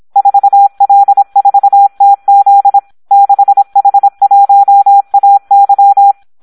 Morse code, heard as a keyed tone, about 26 wpm, 800 Hz. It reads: 4L4TZ 6H1AY
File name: Audio practica CW / SD 1 26